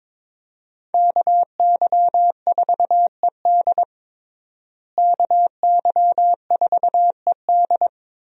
Morse code, heard as a keyed tone, about 21 wpm, 700 Hz. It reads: KY4ED KY4ED
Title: USA Call Signs - 5 Characters - Repeated 2x - 22wpm